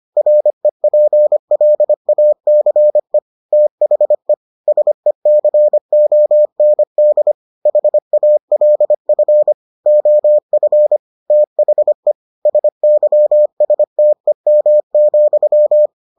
Morse code, heard as a keyed tone, about 25 wpm, 600 Hz. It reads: REPLACE THE SECOND HALF OF THE SYSTEM,